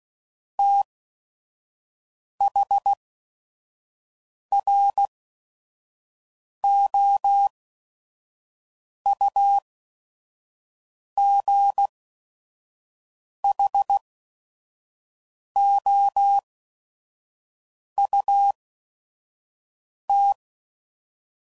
Morse code, heard as a keyed tone, about 16 wpm, spaced as for 5 wpm, 800 Hz. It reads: THROUGHOUT
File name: CW Code Groups @5WPM / Weather Sets / Weather 5 12